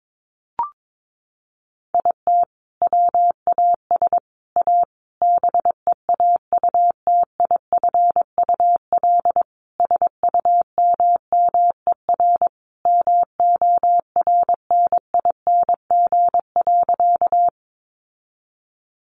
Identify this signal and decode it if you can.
Morse code, heard as a keyed tone, about 22 wpm, 700 Hz. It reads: IT WAS A BEAUTIFUL SUMMER MORNING.
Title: Sentences from Top 1100 Words 22wpm